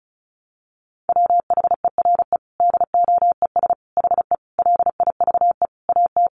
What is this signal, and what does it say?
Morse code, heard as a keyed tone, about 34 wpm, 700 Hz. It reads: WHERE DOES HE LIVE AT
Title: Sentences from Top 800 Words 35wpm